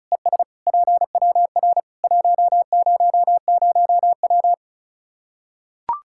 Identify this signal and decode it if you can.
Morse code, heard as a keyed tone, about 34 wpm, 700 Hz. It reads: ES PWR 100W
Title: K1USN SST QSO Practice 35wpm